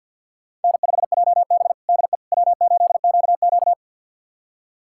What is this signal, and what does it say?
Morse code, heard as a keyed tone, about 49 wpm, 700 Hz. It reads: N5JB DE W8QQ
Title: USA Callsign DE Callsign - 4 Characters - 50wpm